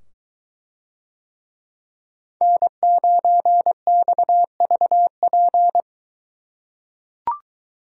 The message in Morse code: N9XVP

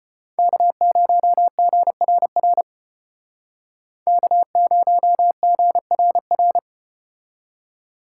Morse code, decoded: K0GRR K0GRR

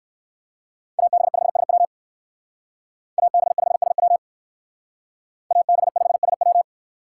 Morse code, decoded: AB5SW AB5SW AB5SW

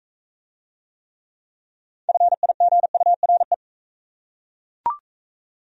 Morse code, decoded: FIGURE

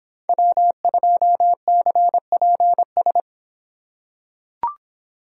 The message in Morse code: W2CPS

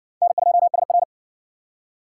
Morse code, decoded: N2SR

